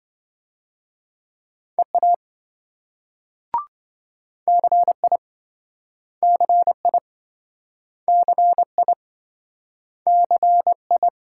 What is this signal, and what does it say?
Morse code, heard as a keyed tone, about 26 wpm, 700 Hz. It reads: EA CI CI CI CI